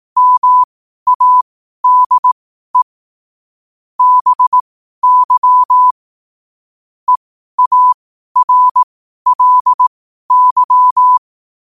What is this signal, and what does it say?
Morse code, heard as a keyed tone, about 18 wpm, spaced as for 12 wpm, 1000 Hz. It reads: MADE BY EARLY
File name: qotdfiles / 12wpm